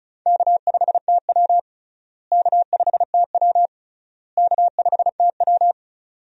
Morse code, heard as a keyed tone, about 36 wpm, 700 Hz. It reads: K5TW K5TW K5TW
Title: USA Call Signs - 4 Characters - Repeated 3x - 35wpm